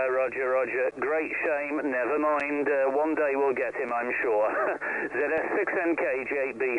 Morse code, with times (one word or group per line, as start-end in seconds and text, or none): none